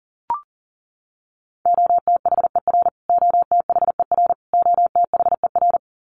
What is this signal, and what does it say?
Morse code, heard as a keyed tone, about 40 wpm, 700 Hz. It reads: OTHER OTHER OTHER
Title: Top 500 Words Repeated 40wpm